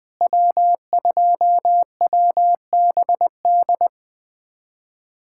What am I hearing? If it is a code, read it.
Morse code, heard as a keyed tone, about 20 wpm, 700 Hz. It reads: W2WBD
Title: USA Callsign DE Callsign - 5 Characters - 20wpm